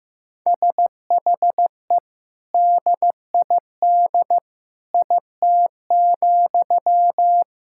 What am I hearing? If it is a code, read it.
Morse code, heard as a keyed tone, about 15 wpm, 700 Hz. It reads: SHE DID IT,
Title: Sentences from Top 1000 Words - Review Part 6 - 15wpm